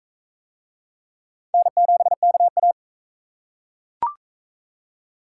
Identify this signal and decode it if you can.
Morse code, heard as a keyed tone, about 36 wpm, 700 Hz. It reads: N7KA